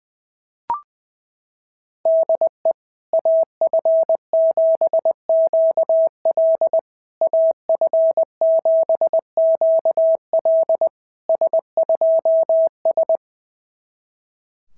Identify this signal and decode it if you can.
Morse code, heard as a keyed tone, about 19 wpm, 660 Hz. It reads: DE AF7QL AF7QL S2S
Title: SOTA QSO Practice 20wpm